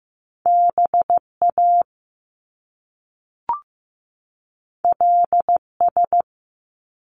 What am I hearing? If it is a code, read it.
Morse code, heard as a keyed tone, about 15 wpm, 700 Hz. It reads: BA LS
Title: Top Two Letter Combinations in Words 15wpm